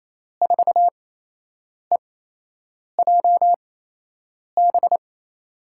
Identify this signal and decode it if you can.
Morse code, heard as a keyed tone, about 28 wpm, spaced as for 8 wpm, 700 Hz. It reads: 4EJB
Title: USA Call Signs - 8x Character Spacing - 28wpm